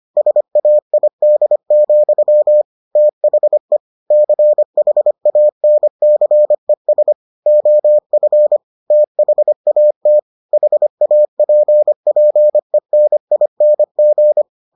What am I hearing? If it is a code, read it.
Morse code, heard as a keyed tone, about 25 wpm, 600 Hz. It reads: SAID, THE CHANCES OF THAT HAPPENING